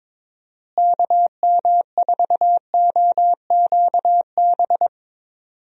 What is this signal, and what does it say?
Morse code, heard as a keyed tone, about 22 wpm, 700 Hz. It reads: KM4OQB
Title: USA Call Signs - 6 Characters - Repeated 2x - 22wpm